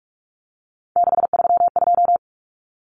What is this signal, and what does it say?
Morse code, heard as a keyed tone, about 44 wpm, 700 Hz. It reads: 632
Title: Common Cut Numbers - 3-Digit 45wpm